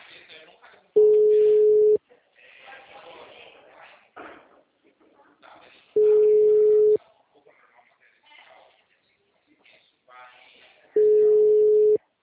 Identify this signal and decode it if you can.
Ringback tone (Europe, 425 Hz), with speech.